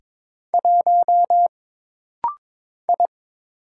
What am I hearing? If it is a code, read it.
Morse code, heard as a keyed tone, about 22 wpm, 700 Hz. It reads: 1 I
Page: This rapid-fire timing is geared to push your ICR (Instant Character Recognition) skills.